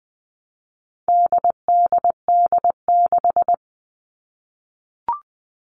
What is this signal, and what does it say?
Morse code, heard as a keyed tone, about 20 wpm, 700 Hz. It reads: DDD6
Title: All Cut Numbers - 4-Digit 20wpm